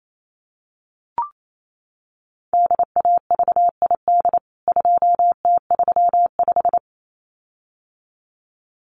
Morse code, heard as a keyed tone, about 28 wpm, 700 Hz. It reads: DAVID 2T35
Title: CWT Exchange 28wpm